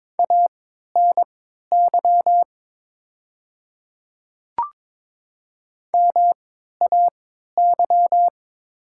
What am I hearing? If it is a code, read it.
Morse code, heard as a keyed tone, about 22 wpm, spaced as for 13 wpm, 690 Hz. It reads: ANY MAY
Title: 3x Character Spacing - Top 500 Words - 22wpm